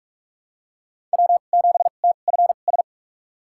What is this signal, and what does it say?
Morse code, heard as a keyed tone, about 41 wpm, spaced as for 31 wpm, 700 Hz. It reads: W7TFS
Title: USA Call Signs - 5 Characters - 2x Character Spacing - 45wpm